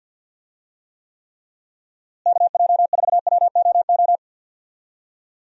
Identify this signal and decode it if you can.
Morse code, heard as a keyed tone, about 52 wpm, 700 Hz. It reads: KJ4WOO
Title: USA Call Signs - 6 Characters - Repeated 3x - 50wpm